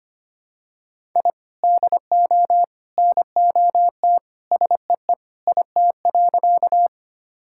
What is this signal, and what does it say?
Morse code, heard as a keyed tone, about 25 wpm, 700 Hz. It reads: I DO NOT SEE IT.
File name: practice / 25wpm / Sentences with 3-Word Phrases - 25wpm